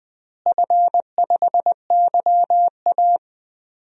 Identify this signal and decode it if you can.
Morse code, heard as a keyed tone, about 21 wpm, 700 Hz. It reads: F5YA
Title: DXCC-100 Call Signs - 4 Characters - Repeated 2x - 20wpm